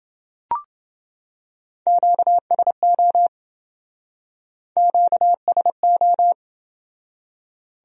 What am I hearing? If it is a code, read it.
Morse code, heard as a keyed tone, about 27 wpm, 700 Hz. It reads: QSO QSO